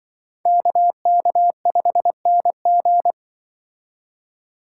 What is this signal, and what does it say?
Morse code, heard as a keyed tone, about 24 wpm, 700 Hz. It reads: KK5NG
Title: Speed Racing - USA 2x2 Call Signs 20wpm